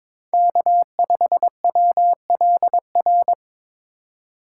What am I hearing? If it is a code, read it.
Morse code, heard as a keyed tone, about 22 wpm, 700 Hz. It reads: K5WLR
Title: USA Call Signs - 5 Characters - 4x Character Spacing - 22wpm